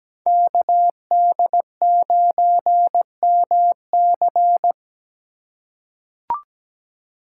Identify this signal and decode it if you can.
Morse code, heard as a keyed tone, about 17 wpm, 700 Hz. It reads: KD9MC
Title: USA Call Signs - 5 Characters - 2x Character Spacing - 17wpm